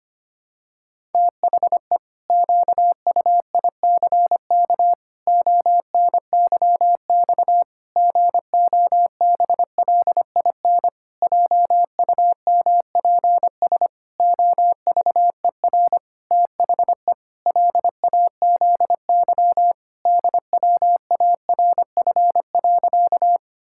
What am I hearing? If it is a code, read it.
Morse code, heard as a keyed tone, about 25 wpm, 700 Hz. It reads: THE QUICK ONYX GOBLIN JUMPS OVER THE LAZY DWARF.
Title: Pangram Challenge v2 - 25wpm